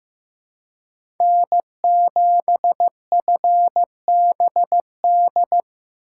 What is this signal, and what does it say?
Morse code, heard as a keyed tone, about 15 wpm, 700 Hz. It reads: N7FBD
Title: USA Call Signs - 5 Characters - 3x Character Spacing - 15wpm